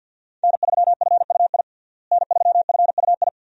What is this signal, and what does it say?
Morse code, heard as a keyed tone, about 52 wpm, 700 Hz. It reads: N3FUI N3FUI
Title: USA Call Signs - 5 Characters - Repeated 2x - 50wpm